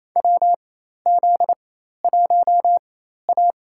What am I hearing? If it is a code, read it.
Morse code, heard as a keyed tone, about 28 wpm, spaced as for 13 wpm, 700 Hz. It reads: WZ1A
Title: USA Call Signs - 4 Characters - 4x Character Spacing - 28wpm